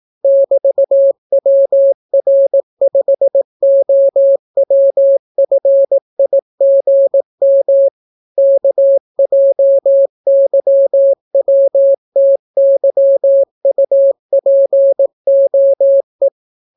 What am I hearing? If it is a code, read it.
Morse code, heard as a keyed tone, about 18 wpm, 550 Hz. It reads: <BT>WR5OWFIGM KJYWTYUPOE